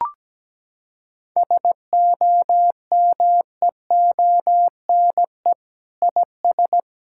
Morse code, heard as a keyed tone, about 17 wpm, 700 Hz. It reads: SOMEONE IS
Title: Sentences from Top 1000 Words - Review Part 6 - 17wpm